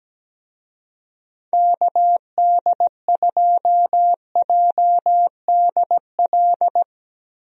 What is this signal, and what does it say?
Morse code, heard as a keyed tone, about 17 wpm, 700 Hz. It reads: KD2JDL
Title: USA Call Signs - 6 Characters - Repeated 3x - 17wpm